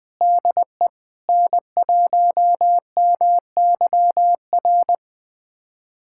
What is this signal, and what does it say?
Morse code, heard as a keyed tone, about 20 wpm, 700 Hz. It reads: DE N1MYR
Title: USA Callsign DE Callsign 20wpm